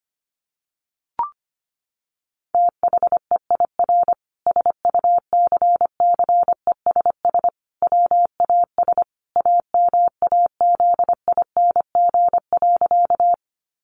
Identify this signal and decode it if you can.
Morse code, heard as a keyed tone, about 25 wpm, 700 Hz. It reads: THEIR SUCCESS WAS AMAZING.
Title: Sentences from Top 1500 Words 25wpm